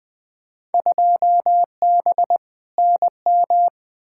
Morse code, heard as a keyed tone, about 21 wpm, 700 Hz. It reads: 2B NM
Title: Field Day Exchanges Encore - 20wpm